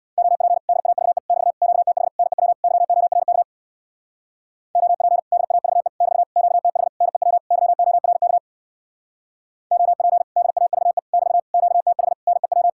Morse code, heard as a keyed tone, about 60 wpm, 700 Hz. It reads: OP DAVE <BT> QTH NEW YORK OP DAVE <BT> QTH NEW YORK OP DAVE <BT> QTH NEW
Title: Speed-Racing Common QSO Phrases Encore - 50wpm